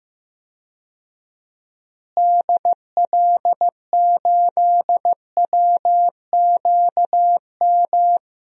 Morse code, heard as a keyed tone, about 15 wpm, 700 Hz. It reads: DL8WQM
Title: DXCC-100 Call Signs - 6 Characters - Repeated 2x - 15wpm